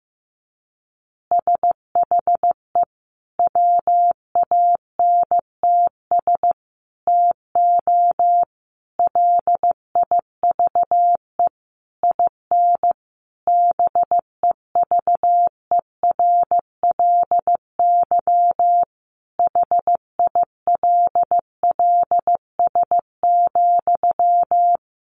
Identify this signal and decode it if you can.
Morse code, heard as a keyed tone, about 15 wpm, 700 Hz. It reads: SHE WANTS TO LIVE IN BEVERLY HILLS,
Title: Easy Phrases 15wpm